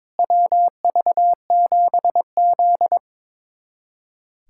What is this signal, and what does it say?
Morse code, heard as a keyed tone, about 22 wpm, 700 Hz. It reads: WV7Z
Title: USA Callsign DE Callsign - 4 Characters - 22wpm